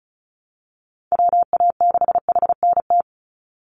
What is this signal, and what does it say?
Morse code, heard as a keyed tone, about 34 wpm, 700 Hz. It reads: WA6HNT
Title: USA Call Signs 35wpm